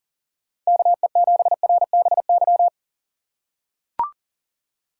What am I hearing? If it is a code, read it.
Morse code, heard as a keyed tone, about 40 wpm, 700 Hz. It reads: KE7RBY
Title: USA Call Signs - 6 Characters - 4x Character Spacing - 40wpm